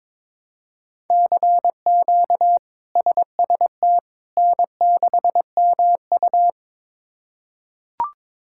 Morse code, heard as a keyed tone, about 22 wpm, 700 Hz. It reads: CQ SST N6MU
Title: K1USN SST QSO Practice - 22wpm